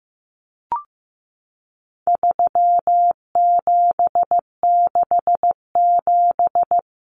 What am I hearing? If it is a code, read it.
Morse code, heard as a keyed tone, about 15 wpm, 700 Hz. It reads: 3767
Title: Numbers - 4-Digit 15wpm